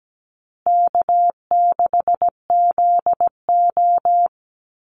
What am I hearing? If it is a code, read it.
Morse code, heard as a keyed tone, about 17 wpm, 700 Hz. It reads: K6ZO
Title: Easy Call Signs Repeated 2x - 17wpm